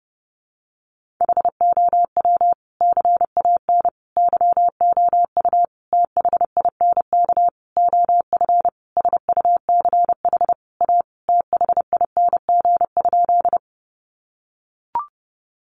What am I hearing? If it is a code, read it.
Morse code, heard as a keyed tone, about 30 wpm, 700 Hz. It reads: HOW CAN YOU THINK OF SUCH A THING?
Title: Sentences from Top 200 Words 30wpm